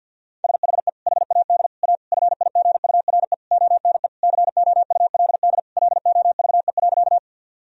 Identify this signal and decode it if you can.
Morse code, heard as a keyed tone, about 51 wpm, 700 Hz. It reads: SHE HAD A FIGURE ONE COULD LOVE.
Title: Sentences from Top 1000 Words - Review Part 2 - 50wpm